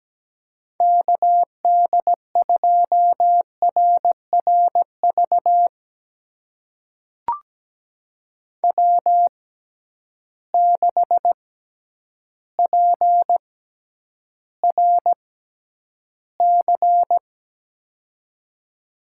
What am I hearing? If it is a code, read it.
Morse code, heard as a keyed tone, about 17 wpm, spaced as for 9 wpm, 700 Hz. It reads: KD2RRV W6PRC
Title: USA Call Signs - 6x Character Spacing - 17wpm